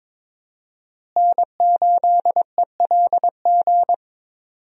Morse code, heard as a keyed tone, about 22 wpm, 700 Hz. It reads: N8ELG
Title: USA Call Signs - 5 Characters - Repeated 3x - 22wpm